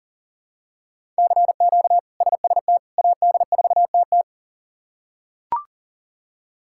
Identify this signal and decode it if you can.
Morse code, heard as a keyed tone, about 40 wpm, 700 Hz. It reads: CQ SST AD4TT
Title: K1USN SST QSO Practice - 40wpm